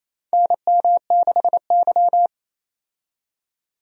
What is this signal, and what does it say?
Morse code, heard as a keyed tone, about 27 wpm, 700 Hz. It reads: NM6Y
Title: USA Call Signs - 4 Characters - 28wpm